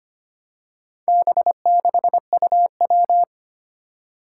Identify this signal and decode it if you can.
Morse code, heard as a keyed tone, about 25 wpm, 700 Hz. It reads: B6UW
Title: Four Challenging Letter-Number Combinations 25wpm